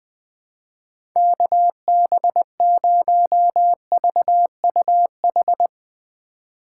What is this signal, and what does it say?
Morse code, heard as a keyed tone, about 20 wpm, 700 Hz. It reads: KB0VUH
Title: USA Call Signs - 6 Characters - 3x Character Spacing - 20wpm